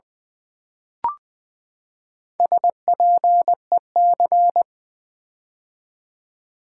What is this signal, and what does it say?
Morse code, heard as a keyed tone, about 20 wpm, 700 Hz. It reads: SPEC
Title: Build A Word - 4 Letters At A Time - 8-Letter Words - 20wpm